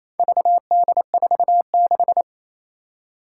Morse code, heard as a keyed tone, about 28 wpm, 700 Hz. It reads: VD46
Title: Four Challenging Letter-Number Combinations 28wpm